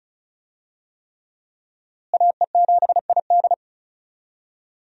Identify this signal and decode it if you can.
Morse code, heard as a keyed tone, about 36 wpm, 700 Hz. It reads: AE7ID
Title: USA Call Signs - 5 Characters - Repeated 3x - 35wpm